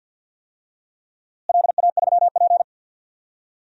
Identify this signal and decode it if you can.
Morse code, heard as a keyed tone, about 52 wpm, 700 Hz. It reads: LA3P
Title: DXCC-100 Call Signs - 4 Characters - Repeated 3x - 50wpm